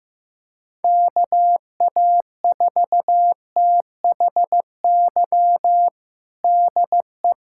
Morse code, heard as a keyed tone, about 15 wpm, 700 Hz. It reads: KA4THY DE
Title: USA Callsign DE Callsign 15wpm